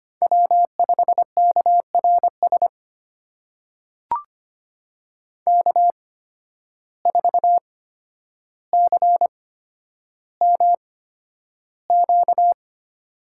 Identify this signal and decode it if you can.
Morse code, heard as a keyed tone, about 25 wpm, spaced as for 11 wpm, 700 Hz. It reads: W5KRS K4CMQ
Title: USA Call Signs - 5 Characters - 8x Character Spacing - 25wpm